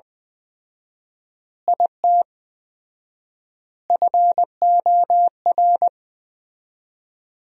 Morse code, heard as a keyed tone, about 20 wpm, 700 Hz. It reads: IT FOR
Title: 4x Word-Spacing - Sentences from Top 200 Words - 20wpm